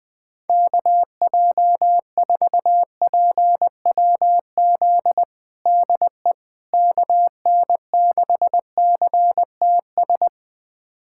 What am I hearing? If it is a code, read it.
Morse code, heard as a keyed tone, about 20 wpm, 700 Hz. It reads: KJ4PWZ DE KN6CTS
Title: USA Callsign DE Callsign 20wpm